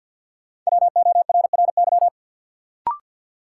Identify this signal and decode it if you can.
Morse code, heard as a keyed tone, about 53 wpm, 700 Hz. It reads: WORRY
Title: Sets of 5 Words - New Word Drill - 50wpm